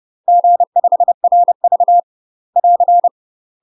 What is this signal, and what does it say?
Morse code, heard as a keyed tone, about 30 wpm, 700 Hz. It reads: G5RV <AR>
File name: UK MASTER QSOs TRAINING / 30WPM / UK MASTER QSOs 0039